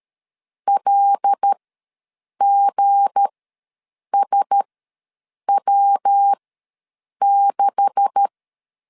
Morse code, heard as a keyed tone, about 13 wpm, spaced as for 7 wpm, 800 Hz. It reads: LGSW6
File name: LEZIONI / 9 2867 ws 12wpm 3431